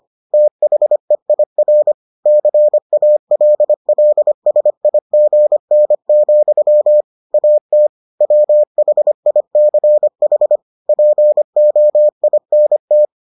Morse code, heard as a keyed tone, about 25 wpm, 600 Hz. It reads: THEIR CALLSIGN, AT WHICH POINT